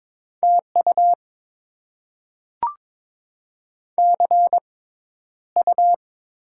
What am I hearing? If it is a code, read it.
Morse code, heard as a keyed tone, about 22 wpm, spaced as for 11 wpm, 690 Hz. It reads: TU CU
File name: Top 20 QSO Elements - 6x Character Spacing - 22wpm